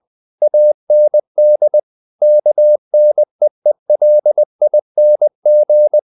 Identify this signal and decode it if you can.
Morse code, heard as a keyed tone, about 20 wpm, 600 Hz. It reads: AND KNEELING
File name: the old man and the sea / the old man and the sea part 1 20wpm